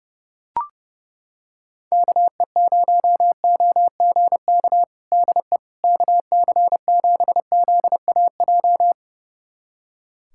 Morse code, heard as a keyed tone, about 32 wpm, 700 Hz. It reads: KE0OGK DE KC7ZAJ
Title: USA Callsign DE Callsign - 6 Characters - 30wpm